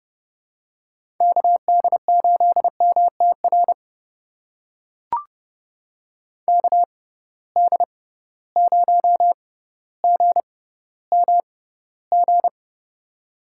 Morse code, heard as a keyed tone, about 31 wpm, spaced as for 15 wpm, 700 Hz. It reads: KD8MTR KD0GMG
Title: USA Call Signs - 6 Characters - 6x Character Spacing - 30wpm